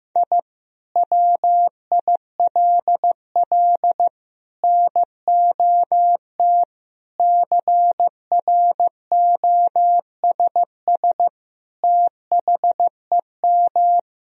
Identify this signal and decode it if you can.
Morse code, heard as a keyed tone, about 15 wpm, 700 Hz. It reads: I WILL NOT CROSS THEM
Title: Sentences from Top 1000 Words - Review Part 1 - 15wpm